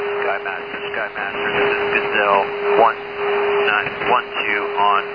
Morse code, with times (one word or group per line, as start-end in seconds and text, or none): none